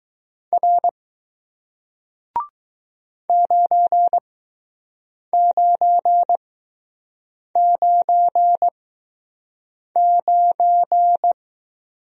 Morse code, heard as a keyed tone, about 20 wpm, spaced as for 6 wpm, 700 Hz. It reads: R 9999